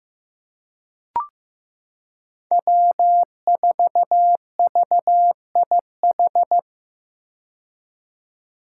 Morse code, heard as a keyed tone, about 15 wpm, 700 Hz. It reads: W4VIH
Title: Code Groups - 5 Letter:Numbers - 15wpm